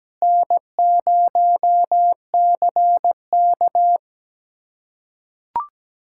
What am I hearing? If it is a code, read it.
Morse code, heard as a keyed tone, about 17 wpm, 700 Hz. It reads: N0CK
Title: USA Callsign DE Callsign - 4 Characters - 17wpm